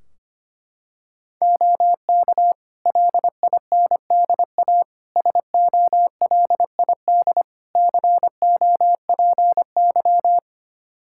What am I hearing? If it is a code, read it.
Morse code, heard as a keyed tone, about 25 wpm, 700 Hz. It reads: OK LINDA SOLID COPY
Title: Practice QSO - Exchange 3 - More Info - 25wpm